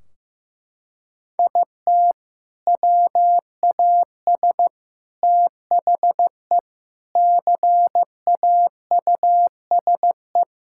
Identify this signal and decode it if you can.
Morse code, heard as a keyed tone, about 15 wpm, 700 Hz. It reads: IT WAS THE CAUSE
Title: Sentences from Top 1600 Words 15wpm